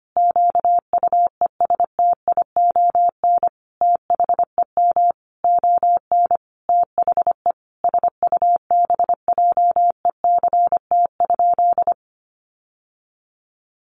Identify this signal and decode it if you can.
Morse code, heard as a keyed tone, about 25 wpm, 700 Hz. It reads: QUESTION THEM ON THE SUBJECT?
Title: Sentences from Top 800 Words 25wpm